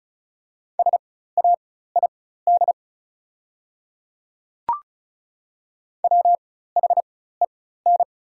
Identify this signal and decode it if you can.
Morse code, heard as a keyed tone, about 34 wpm, spaced as for 16 wpm, 690 Hz. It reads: SAID WHEN
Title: 4x Character Spacing - Top 300 Words - 35wpm